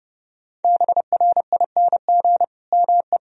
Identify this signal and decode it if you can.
Morse code, heard as a keyed tone, about 30 wpm, 700 Hz. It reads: BRING ME
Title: Sentences from Top 1000 Words - Review Part 3 - 30wpm